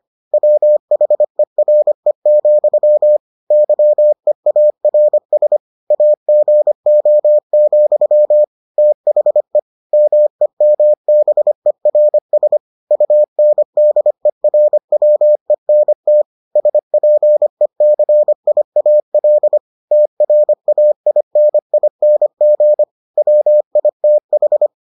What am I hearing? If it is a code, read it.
Morse code, heard as a keyed tone, about 25 wpm, 600 Hz. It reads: WHERE, YEARS AGO, THE MEMBERS UNDERWENT SPECIAL TRAINING WITH